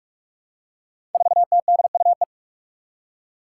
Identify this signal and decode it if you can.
Morse code, heard as a keyed tone, about 44 wpm, 700 Hz. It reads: 4TDUE
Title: Five Challenging Letter-Number Combinations 45wpm